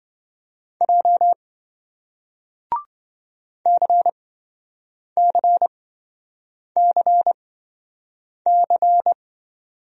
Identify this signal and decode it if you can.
Morse code, heard as a keyed tone, about 28 wpm, spaced as for 7 wpm, 700 Hz. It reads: J CCCC